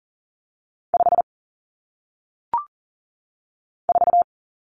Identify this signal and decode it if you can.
Morse code, heard as a keyed tone, about 39 wpm, 700 Hz. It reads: 5 4
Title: Numbers - 1-Digit 40wpm